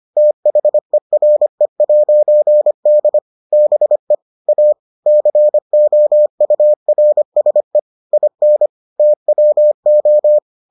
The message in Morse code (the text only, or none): THERE'D BE A COURSE IN TWO